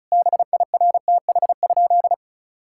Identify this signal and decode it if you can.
Morse code, heard as a keyed tone, about 34 wpm, 700 Hz. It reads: BIRTH?
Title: Sentences from Top 1000 Words - Review Part 5 - 35wpm